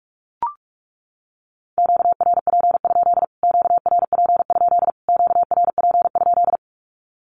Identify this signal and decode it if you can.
Morse code, heard as a keyed tone, about 44 wpm, 700 Hz. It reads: QRP? QRP? QRP?
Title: Q-Codes Repeated 3x 45wpm